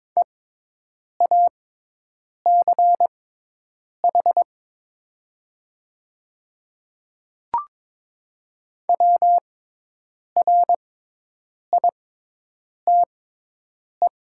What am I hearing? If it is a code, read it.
Morse code, heard as a keyed tone, about 22 wpm, spaced as for 8 wpm, 700 Hz. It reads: EACH WRITE
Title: 6x Character Spacing - Top 100 Words - 22wpm